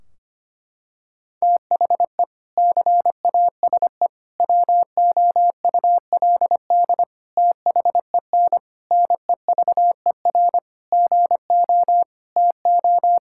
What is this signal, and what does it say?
Morse code, heard as a keyed tone, about 25 wpm, 700 Hz. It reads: THE CASE WOULD THEN NEVER GO TO
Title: Sentences from Top 1600 Words 25wpm